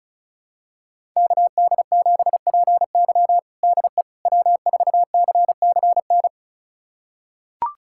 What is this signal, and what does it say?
Morse code, heard as a keyed tone, about 34 wpm, 700 Hz. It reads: KD7PY DE W4CCN
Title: USA Callsign DE Callsign - 5 Characters - 35wpm